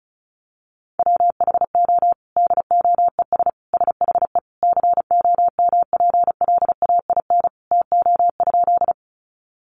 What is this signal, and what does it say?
Morse code, heard as a keyed tone, about 34 wpm, 700 Hz. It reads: WHO DOES SHE COMPLAIN TO?
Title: Sentences from Top 1300 Words 35wpm